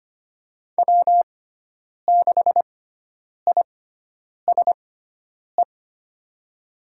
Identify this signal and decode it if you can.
Morse code, heard as a keyed tone, about 25 wpm, spaced as for 9 wpm, 700 Hz. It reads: W6ISE
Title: USA Call Signs - 6x Character Spacing - 25wpm